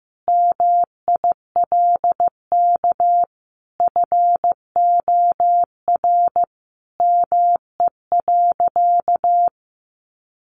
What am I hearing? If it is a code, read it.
Morse code, heard as a keyed tone, about 15 wpm, 700 Hz. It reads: MILK FOR ME.
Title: Sentences from Top 1500 Words 15wpm